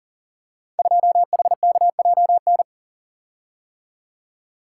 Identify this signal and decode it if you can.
Morse code, heard as a keyed tone, about 40 wpm, 700 Hz. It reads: 2HKJN
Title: Code Groups - 5 Letter:Numbers - 40wpm